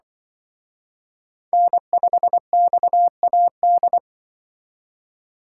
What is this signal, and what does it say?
Morse code, heard as a keyed tone, about 24 wpm, 700 Hz. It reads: N5XAD
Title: Speed Racing - USA 1x3 Call Signs 20wpm